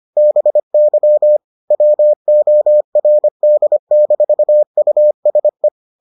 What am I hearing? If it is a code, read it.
Morse code, heard as a keyed tone, about 25 wpm, 600 Hz. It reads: BY WORD-USE